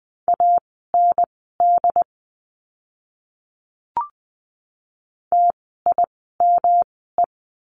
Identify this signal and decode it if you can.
Morse code, heard as a keyed tone, about 20 wpm, spaced as for 15 wpm, 700 Hz. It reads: AND TIME
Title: 115 Lesson 19f - Letter w - Words 20wpm